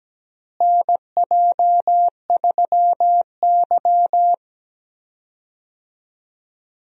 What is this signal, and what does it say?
Morse code, heard as a keyed tone, about 17 wpm, 700 Hz. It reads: NJ3Y
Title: USA Call Signs - 4 Characters - Repeated 3x - 17wpm